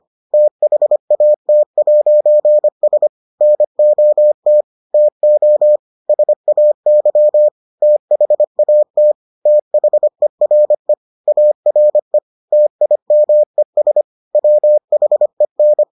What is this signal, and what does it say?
Morse code, heard as a keyed tone, about 25 wpm, 600 Hz. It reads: THAT'S NOT TO SAY THAT THERE ARE TIMES WHEN